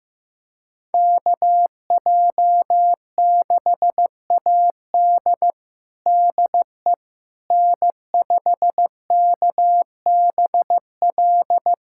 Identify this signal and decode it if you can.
Morse code, heard as a keyed tone, about 15 wpm, 700 Hz. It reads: KJ6AD DE N5KBL
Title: USA Callsign DE Callsign - 5 Characters - 15wpm